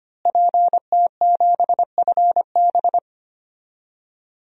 Morse code, heard as a keyed tone, about 25 wpm, 700 Hz. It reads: PT7FB
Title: DXCC-100 Call Signs - 5 Characters - Repeated 2x - 25wpm